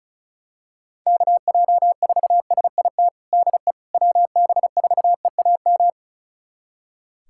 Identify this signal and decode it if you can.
Morse code, heard as a keyed tone, about 35 wpm, 700 Hz. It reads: KJ4SIT DE WB4EAM
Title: USA Callsign DE Callsign - 6 Characters - 35wpm